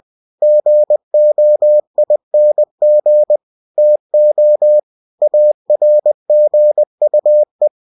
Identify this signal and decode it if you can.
Morse code, heard as a keyed tone, about 20 wpm, 600 Hz. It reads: GOING TO ARGUE
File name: cw-books / winnie-the-pooh / winnie-20wpm-page-0037